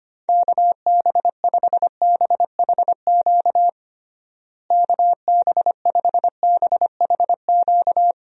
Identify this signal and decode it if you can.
Morse code, heard as a keyed tone, about 25 wpm, 700 Hz. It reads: KB5BHQ KB5BHQ
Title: USA Call Signs - 6 Characters - Repeated 3x - 25wpm